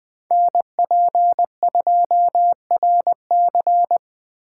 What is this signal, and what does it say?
Morse code, heard as a keyed tone, about 20 wpm, 700 Hz. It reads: NP2RC